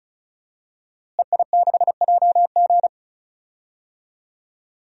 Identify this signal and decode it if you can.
Morse code, heard as a keyed tone, about 36 wpm, 700 Hz. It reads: EI6JG
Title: DXCC-100 Call Signs - 5 Characters - Repeated 3x - 35wpm